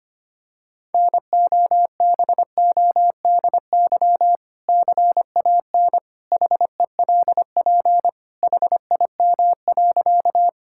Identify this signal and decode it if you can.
Morse code, heard as a keyed tone, about 25 wpm, 700 Hz. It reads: NOBODY CAN HELP HIM.
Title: Sentences from Top 1600 Words 25wpm